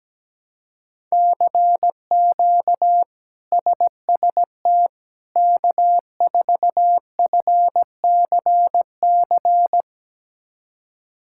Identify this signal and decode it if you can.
Morse code, heard as a keyed tone, about 17 wpm, 700 Hz. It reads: CQ SST K4FCC
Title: K1USN SST QSO Practice - 17wpm